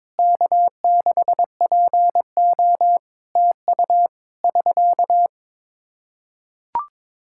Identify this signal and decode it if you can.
Morse code, heard as a keyed tone, about 22 wpm, 700 Hz. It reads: K6PO TU <SK>
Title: Practice QSO - Exchange 4 - Ending - 22wpm